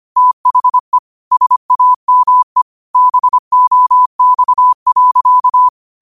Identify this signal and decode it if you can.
Morse code, heard as a keyed tone, about 25 wpm, 1000 Hz. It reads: THE SAME BOX.
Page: New quotes every day in morse code at 25 Words per minute.